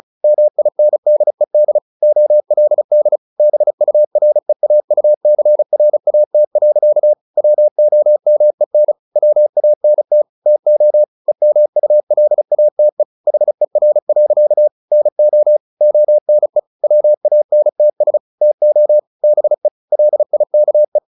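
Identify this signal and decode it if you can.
Morse code, heard as a keyed tone, about 34 wpm, 600 Hz. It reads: MINDED OLD BUREAUCRAT. WOMEN WANT TO EMULATE HER. NO ONE WANTS TO BE LIKE
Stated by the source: Morse practice at 35 wpm from CW med Gnister, sending Ken Follett's novel Never in English.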